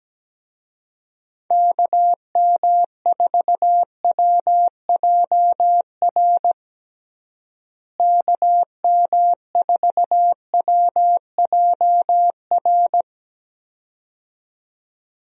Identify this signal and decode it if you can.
Morse code, heard as a keyed tone, about 17 wpm, 700 Hz. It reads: KM4WJR KM4WJR
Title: USA Call Signs - 6 Characters - Repeated 2x - 17wpm